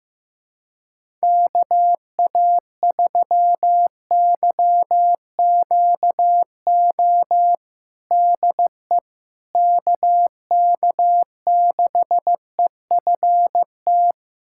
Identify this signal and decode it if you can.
Morse code, heard as a keyed tone, about 15 wpm, 700 Hz. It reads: KA3YQO DE KK6EFT
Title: USA Callsign DE Callsign - 6 Characters - 15wpm